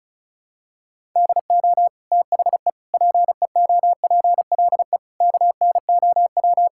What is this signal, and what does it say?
Morse code, heard as a keyed tone, about 36 wpm, 700 Hz. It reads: DO THE PEOPLE KNOW
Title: Sentences from Top 1000 Words - Review Part 5 - 35wpm